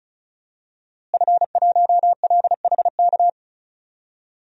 Morse code, heard as a keyed tone, about 34 wpm, 700 Hz. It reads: F1LHK
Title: DXCC-100 Call Signs - 5 Characters - Repeated 2x - 35wpm